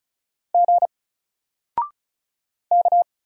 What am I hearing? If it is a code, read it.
Morse code, heard as a keyed tone, about 33 wpm, 700 Hz. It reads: G K
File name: Single-Letter-Number-35wpm